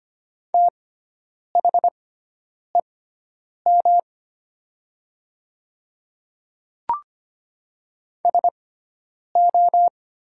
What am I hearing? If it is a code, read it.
Morse code, heard as a keyed tone, about 25 wpm, spaced as for 9 wpm, 700 Hz. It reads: THEM SO
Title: 6x Character Spacing - Top 100 Words - 25wpm